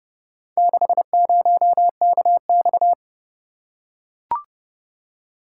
Morse code, heard as a keyed tone, about 27 wpm, 700 Hz. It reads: 60KX